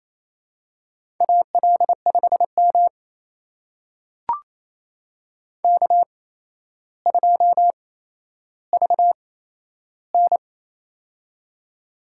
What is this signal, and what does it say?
Morse code, heard as a keyed tone, about 27 wpm, spaced as for 12 wpm, 700 Hz. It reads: AL5M K2VN
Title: USA Call Signs - 4 Characters - 8x Character Spacing - 28wpm